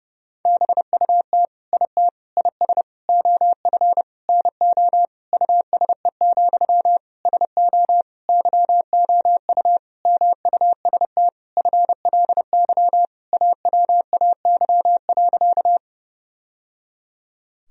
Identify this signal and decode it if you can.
Morse code, heard as a keyed tone, about 32 wpm, 700 Hz. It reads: BUT IT IS OF NO USE, SO YOU MUST FLY AWAY.
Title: Sentences from Top 1000 Words - Review Part 2 - 30wpm